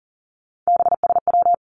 Morse code, heard as a keyed tone, about 41 wpm, 700 Hz. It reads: BSW
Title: Three Challenging Letter-Number Combinations 40wpm